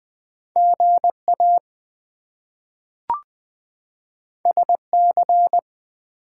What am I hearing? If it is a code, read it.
Morse code, heard as a keyed tone, about 20 wpm, 700 Hz. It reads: GA SC
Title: US States and Canadian Provinces 20wpm